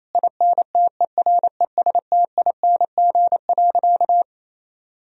Morse code, heard as a keyed tone, about 28 wpm, 700 Hz. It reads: INTERESTING.
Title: Sentences from Top 1000 Words - Review Part 4 - 28wpm